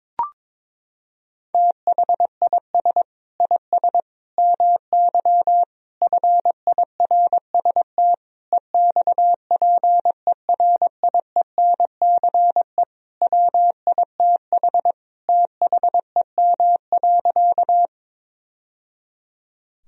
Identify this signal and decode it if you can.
Morse code, heard as a keyed tone, about 22 wpm, 700 Hz. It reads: THIS IS MY FIRST EXPERIENCE WITH THEM.
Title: Sentences from Top 1000 Words - Review Part 3 - 22wpm